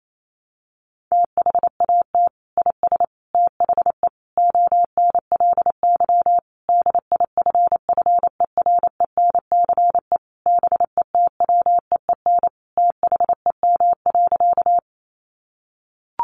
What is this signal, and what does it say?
Morse code, heard as a keyed tone, about 28 wpm, 700 Hz. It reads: THAT IS THE ONLY DIFFERENCE BETWEEN THEM.
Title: Sentences from Top 800 Words 28wpm